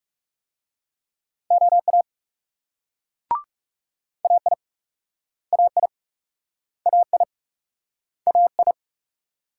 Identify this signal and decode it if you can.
Morse code, heard as a keyed tone, about 39 wpm, 700 Hz. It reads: OA AI AI AI AI